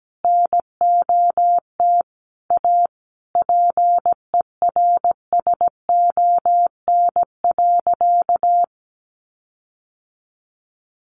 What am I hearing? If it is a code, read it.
Morse code, heard as a keyed tone, about 17 wpm, 700 Hz. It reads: NOT A PERSON.
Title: Sentences from Top 1500 Words 17wpm